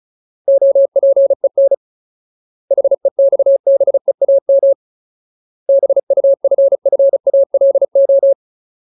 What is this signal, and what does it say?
Morse code, heard as a keyed tone, about 34 wpm, 550 Hz. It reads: OPEN HEXBEAM BUFFALO
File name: Ultimate Word List / 35WPM Ultimate